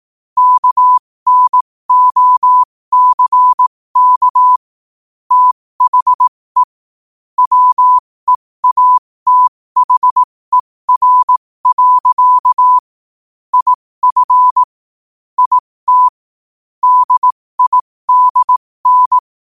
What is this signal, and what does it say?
Morse code, heard as a keyed tone, about 18 wpm, spaced as for 15 wpm, 1000 Hz. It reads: KNOCK THE WEATHER. IF IT DIDN
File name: qotdfiles / 15wpm